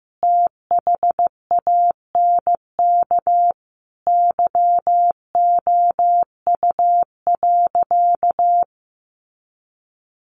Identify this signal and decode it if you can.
Morse code, heard as a keyed tone, about 15 wpm, 700 Hz. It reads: THANK YOU.
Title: Sentences from Top 1100 Words 15wpm